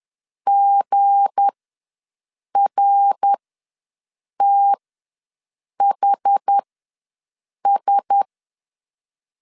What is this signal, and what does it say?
Morse code, heard as a keyed tone, about 10 wpm, spaced as for 6 wpm, 800 Hz. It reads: GRTHS
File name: LEZIONI / 7 GF9Z / 7 GF9 ws 10wpm 4034